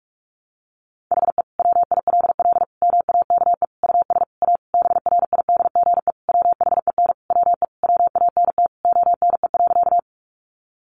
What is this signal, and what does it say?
Morse code, heard as a keyed tone, about 44 wpm, 700 Hz. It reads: HE WILL MAKE US A BRIDGE WHEN WE WANT ONE.
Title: Sentences from Top 1200 Words 45wpm